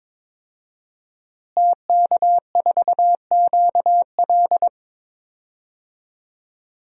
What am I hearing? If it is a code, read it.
Morse code, heard as a keyed tone, about 22 wpm, 700 Hz. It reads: TK4QL
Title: DXCC-100 Call Signs - 5 Characters - Repeated 3x - 22wpm